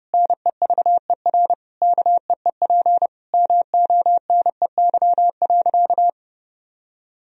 Morse code, heard as a keyed tone, about 30 wpm, 700 Hz. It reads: NEVER KEEP MONEY.
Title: Sentences from Top 1000 Words - Review Part 4 - 30wpm